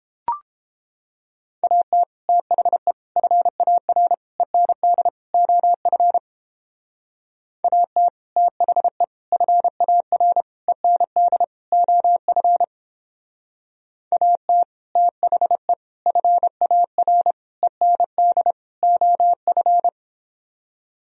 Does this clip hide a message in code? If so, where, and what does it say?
Yes: Morse code from 1.6 to 19.9 s, heard as a keyed tone, about 29 wpm, 700 Hz. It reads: AT THE FAR END OF AT THE FAR END OF AT THE FAR END OF